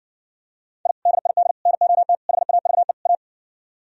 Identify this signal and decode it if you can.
Morse code, heard as a keyed tone, about 70 wpm, 700 Hz. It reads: I DID NOT HAVE A